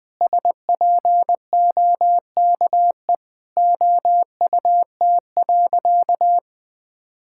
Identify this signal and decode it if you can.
Morse code, heard as a keyed tone, about 20 wpm, 700 Hz. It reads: SPOKE OUT.
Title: Sentences with 7-Word Phrases - 20wpm